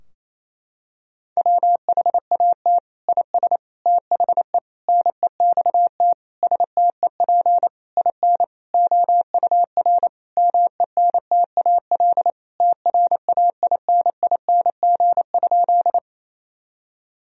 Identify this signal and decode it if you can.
Morse code, heard as a keyed tone, about 28 wpm, 700 Hz. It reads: WHAT IS THE NEXT STEP IN OUR MENTAL TRAINING?
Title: Sentences from Top 1000 Words - Review Part 4 - 28wpm